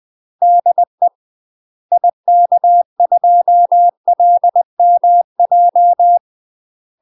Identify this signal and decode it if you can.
Morse code, heard as a keyed tone, about 20 wpm, 700 Hz. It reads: DE IK2LMJ
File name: UK MASTER QSOs TRAINING / 20WPM / UK MASTER QSOs 0045